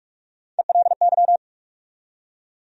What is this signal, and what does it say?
Morse code, heard as a keyed tone, about 46 wpm, 700 Hz. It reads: ELY
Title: USA Call Sign Postfixes - 3 Letters 45wpm